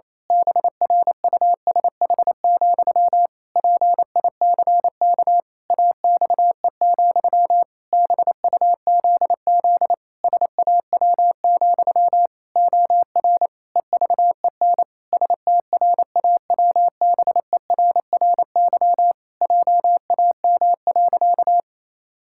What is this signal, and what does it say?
Morse code, heard as a keyed tone, about 28 wpm, 700 Hz. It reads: BRUSH, PICK AXE, BUZZ SAW, OR EVEN STRAWBERRY JAM.
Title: Pangram Challenge v2 - 28wpm